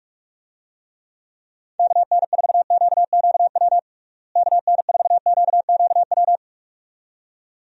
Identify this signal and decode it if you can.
Morse code, heard as a keyed tone, about 44 wpm, 700 Hz. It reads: KN4QQW KN4QQW
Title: USA Call Signs - 6 Characters - Repeated 2x - 45wpm